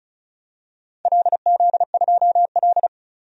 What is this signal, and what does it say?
Morse code, heard as a keyed tone, about 36 wpm, 700 Hz. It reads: LZ2L
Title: DXCC-100 Call Signs - 4 Characters - 35wpm